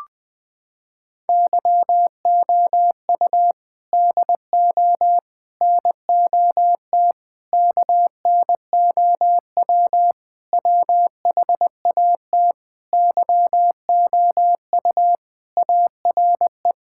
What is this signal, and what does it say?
Morse code, heard as a keyed tone, about 20 wpm, 700 Hz. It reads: YOU DO NOT KNOW WHAT YOU ARE
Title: Sentences with 5-Word Phrases - 20wpm